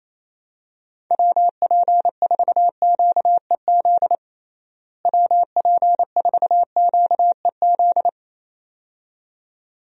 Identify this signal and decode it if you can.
Morse code, heard as a keyed tone, about 28 wpm, 700 Hz. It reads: WP4QEZ WP4QEZ
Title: USA Call Signs - 6 Characters - Repeated 2x - 28wpm